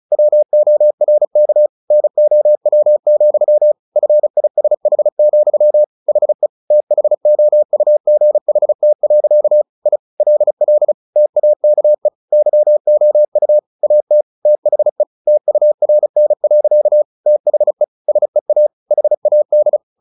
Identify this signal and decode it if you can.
Morse code, heard as a keyed tone, about 34 wpm, 600 Hz. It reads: WORK NOW, FISH, HE THOUGHT. I LL TAKE YOU AT THE TURN. THE SEA HAD
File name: the old man and the sea 2 35wpm